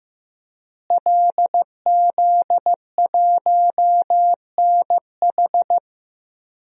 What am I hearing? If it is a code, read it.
Morse code, heard as a keyed tone, about 15 wpm, 700 Hz. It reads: LZ1NH
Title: DXCC-100 Call Signs - 5 Characters - Repeated 2x - 15wpm